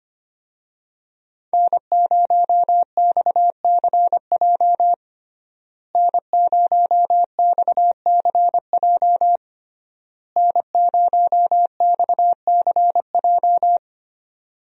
Morse code, heard as a keyed tone, about 25 wpm, 700 Hz. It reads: N0XCJ N0XCJ N0XCJ
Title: USA Call Signs - 5 Characters - Repeated 3x - 25wpm